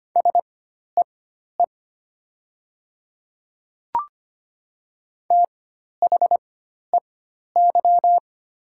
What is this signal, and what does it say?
Morse code, heard as a keyed tone, about 25 wpm, spaced as for 12 wpm, 700 Hz. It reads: SEE THEY
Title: 4x Character Spacing - Top 200 Words - 25wpm